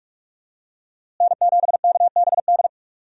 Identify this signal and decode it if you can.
Morse code, heard as a keyed tone, about 46 wpm, 700 Hz. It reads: N7KBD
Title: USA Call Signs - 5 Characters - Repeated 3x - 45wpm